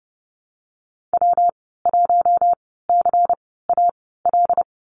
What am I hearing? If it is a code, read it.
Morse code, heard as a keyed tone, about 30 wpm, spaced as for 17 wpm, 700 Hz. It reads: W1CAL
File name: Lesson 19g - Letter w - Call Signs 30wpm